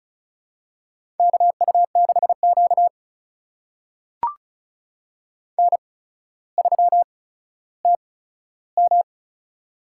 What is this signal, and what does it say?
Morse code, heard as a keyed tone, about 36 wpm, spaced as for 15 wpm, 700 Hz. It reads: KU6Q N3TM
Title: USA Call Signs - 4 Characters - 8x Character Spacing - 35wpm